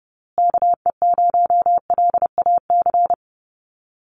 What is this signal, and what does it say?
Morse code, heard as a keyed tone, about 30 wpm, 700 Hz. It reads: KE0LAC
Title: USA Call Signs Repeated 3x 30wpm